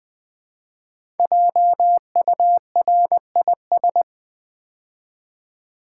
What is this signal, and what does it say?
Morse code, heard as a keyed tone, about 20 wpm, 700 Hz. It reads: JURIS
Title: Build A Word - 5 Letters At A Time - 12-Letter Words - 20wpm